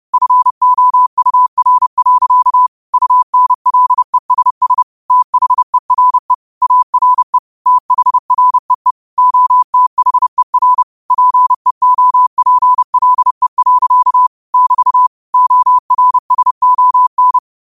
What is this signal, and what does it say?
Morse code, heard as a keyed tone, about 30 wpm, 1000 Hz. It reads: FOUR. UNLESS THERE ARE THREE OTHER PEOPLE. <BT> ORSON
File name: qotdfiles / 30wpm